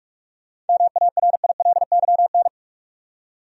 Morse code, heard as a keyed tone, about 42 wpm, 700 Hz. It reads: MARILYN
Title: USA Common Names - 45wpm